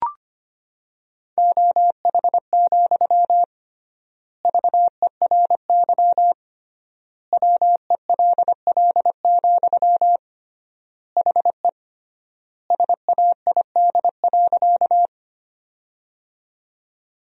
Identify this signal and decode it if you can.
Morse code, heard as a keyed tone, about 25 wpm, 700 Hz. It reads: OH, VERY WELL, HE SAID.
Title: 3x Word-Spacing - Sentences from Top 500 Words - 25wpm